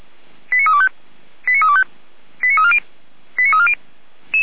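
Five-tone selective call (ZVEI), heard as quick runs of five tones, tones of about 70 ms. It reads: 89325 89325 89340 89340
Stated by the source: ZVEI five-tone selective call received by SDR, fire brigade alerting of Feuerwehr Orlamünde.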